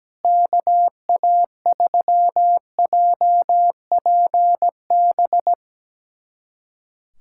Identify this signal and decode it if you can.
Morse code, heard as a keyed tone, about 17 wpm, 700 Hz. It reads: KA3JPB
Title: USA Callsign DE Callsign - 6 Characters - 17wpm